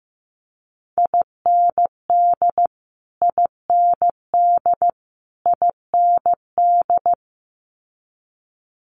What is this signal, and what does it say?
Morse code, heard as a keyed tone, about 15 wpm, 700 Hz. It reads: IND IND IND
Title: Multi-letter Phonemes Repeated 3x 15wpm